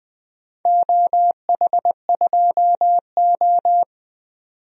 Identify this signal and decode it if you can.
Morse code, heard as a keyed tone, about 20 wpm, 700 Hz. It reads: OH2O
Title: DXCC-100 Call Signs - 4 Characters - 20wpm